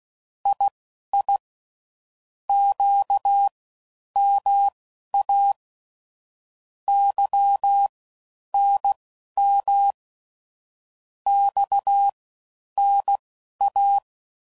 Morse code, heard as a keyed tone, about 16 wpm, spaced as for 10 wpm, 800 Hz. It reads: II QMA YNM XNA
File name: Mixed / CW Comparisons / CW Comparisons Broken Characters